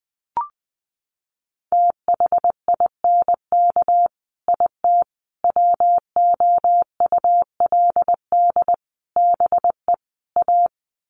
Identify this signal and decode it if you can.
Morse code, heard as a keyed tone, about 20 wpm, 700 Hz. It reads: THINK IT WOULD BE A
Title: Sets of 5 Words Encore2 20wpm